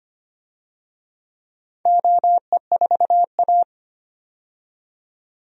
Morse code, heard as a keyed tone, about 25 wpm, 690 Hz. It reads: OE4A
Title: DXCC-100 Call Signs - 4 Characters - 25wpm